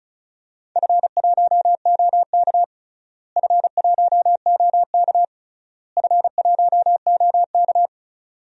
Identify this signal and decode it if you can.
Morse code, heard as a keyed tone, about 36 wpm, 700 Hz. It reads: F1OK F1OK F1OK
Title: DXCC-100 Call Signs - 4 Characters - Repeated 3x - 35wpm